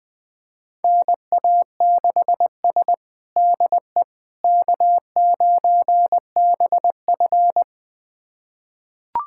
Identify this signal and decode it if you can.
Morse code, heard as a keyed tone, about 20 wpm, 700 Hz. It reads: NA6S DE K9BF
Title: USA Callsign DE Callsign - 4 Characters - 20wpm